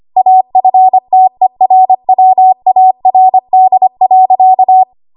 Morse code, heard as a keyed tone, about 25 wpm, 750 Hz. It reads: AFTERWARD.
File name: Archive / 25 WPM / 25WPM